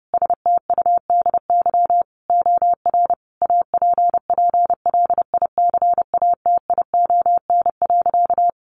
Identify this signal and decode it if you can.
Morse code, heard as a keyed tone, about 28 wpm, 700 Hz. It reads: STUDY OR APPLICATION.
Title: Sentences from Top 1200 Words 30wpm